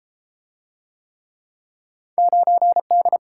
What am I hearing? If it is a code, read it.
Morse code, heard as a keyed tone, about 31 wpm, 700 Hz. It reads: 9D